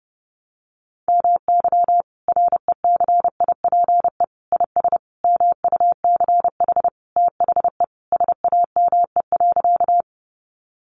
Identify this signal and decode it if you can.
Morse code, heard as a keyed tone, about 30 wpm, 700 Hz. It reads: MY RECIPE IS MUCH THE SAME.
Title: Sentences from Top 1200 Words 30wpm